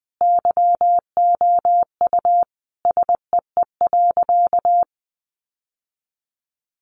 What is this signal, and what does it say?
Morse code, heard as a keyed tone, about 20 wpm, 700 Hz. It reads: YOU SEE.
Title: Sentences from Top 600 Words 20wpm